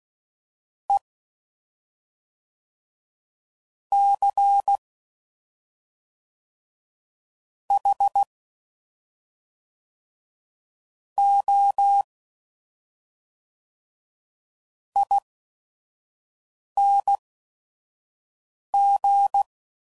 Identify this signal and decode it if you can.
Morse code, heard as a keyed tone, about 16 wpm, spaced as for 5 wpm, 800 Hz. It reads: E C H O ING
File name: Suffixes / Special ING's / SlowToFaster ING's